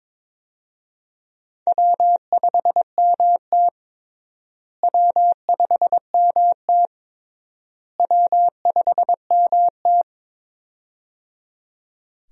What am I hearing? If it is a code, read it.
Morse code, heard as a keyed tone, about 22 wpm, 700 Hz. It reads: W5MT W5MT W5MT
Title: USA Call Signs - 4 Characters - Repeated 3x - 22wpm